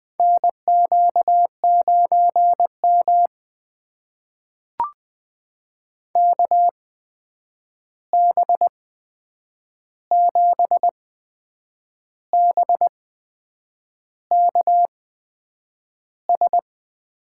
Morse code, heard as a keyed tone, about 20 wpm, spaced as for 8 wpm, 700 Hz. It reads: NQ9M KB7BKS
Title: USA Call Signs - 8x Character Spacing - 20wpm